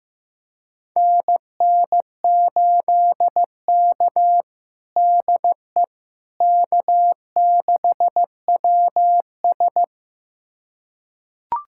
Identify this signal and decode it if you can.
Morse code, heard as a keyed tone, about 15 wpm, 700 Hz. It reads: NN8K DE K6WS
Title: USA Callsign DE Callsign - 4 Characters - 15wpm